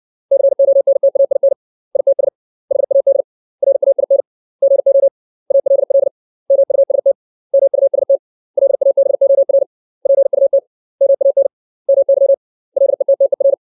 Morse code, heard as a keyed tone, about 60 wpm, 550 Hz. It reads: CONTAIN ITS HAD RAIN GO ALL MUST MUST LABOR PUT MAN MY LETTER
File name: 632 words / 632 Words 60WPM